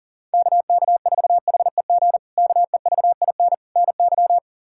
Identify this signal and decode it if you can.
Morse code, heard as a keyed tone, about 40 wpm, 700 Hz. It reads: KK4HEG KEVIN NY
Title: K1USN SST QSO Practice - 40wpm